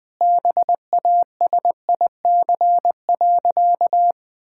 Morse code, heard as a keyed tone, about 20 wpm, 700 Hz. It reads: BASIC.
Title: Sentences with 3-Word Phrases - 20wpm